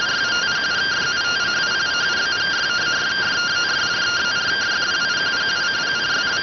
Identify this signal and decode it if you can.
Two-tone teleprinter signal, about 170 Hz shift.